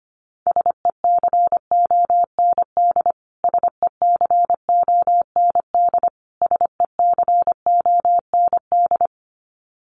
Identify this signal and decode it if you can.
Morse code, heard as a keyed tone, about 25 wpm, 700 Hz. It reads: SECOND SECOND SECOND
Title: Top 500 Words Repeated 25wpm